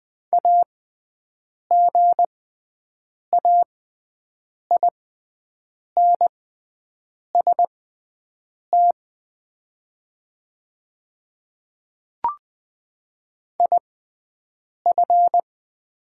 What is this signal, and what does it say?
Morse code, heard as a keyed tone, about 20 wpm, spaced as for 7 wpm, 700 Hz. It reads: AGAINST IF
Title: 6x Character Spacing - Top 500 Words - 20wpm